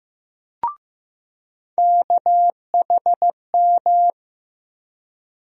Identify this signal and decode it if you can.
Morse code, heard as a keyed tone, about 15 wpm, 700 Hz. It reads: KHM
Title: USA Call Sign Postfixes - 3 Letters 15wpm